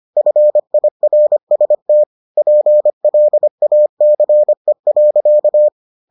Morse code, heard as a keyed tone, about 25 wpm, 600 Hz. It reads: FIRST PLACE.